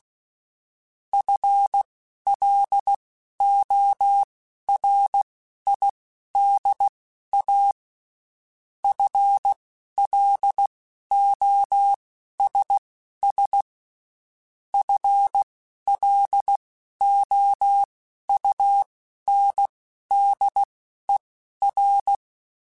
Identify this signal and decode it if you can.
Morse code, heard as a keyed tone, about 16 wpm, spaced as for 11 wpm, 800 Hz. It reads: FLORIDA FLOSS FLOUNDER
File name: Spaced / FL Runs sp mp3s / FL Runs sp @15